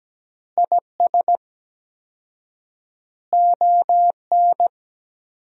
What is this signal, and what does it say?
Morse code, heard as a keyed tone, about 17 wpm, 700 Hz. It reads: IS ON
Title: 4x Word-Spacing - Sentences from Top 400 Words - 17wpm